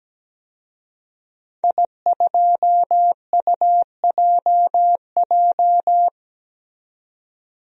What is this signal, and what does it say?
Morse code, heard as a keyed tone, about 17 wpm, 700 Hz. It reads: I2UJJ
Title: DXCC-100 Call Signs - 5 Characters - 17wpm